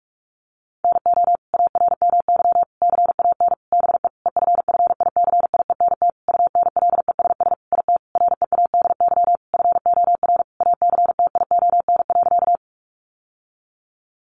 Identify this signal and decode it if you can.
Morse code, heard as a keyed tone, about 44 wpm, 700 Hz. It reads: NO ARMY CAN BE EFFICIENT UNLESS IT READY FOR ACTION.
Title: Sentences from Top 1300 Words 45wpm